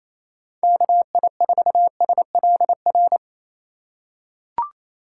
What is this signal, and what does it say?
Morse code, heard as a keyed tone, about 28 wpm, 700 Hz. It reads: KI4SLR
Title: USA Call Signs - 6 Characters - 2x Character Spacing - 28wpm